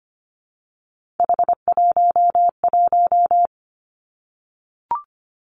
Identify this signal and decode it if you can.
Morse code, heard as a keyed tone, about 25 wpm, 700 Hz. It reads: H11
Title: Three Challenging Letter-Number Combinations 25wpm